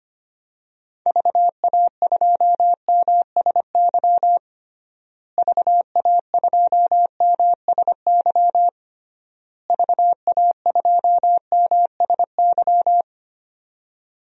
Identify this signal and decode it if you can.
Morse code, heard as a keyed tone, about 25 wpm, 700 Hz. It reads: VA2MSY VA2MSY VA2MSY
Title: DXCC-100 Call Signs - 6 Characters - Repeated 3x - 25wpm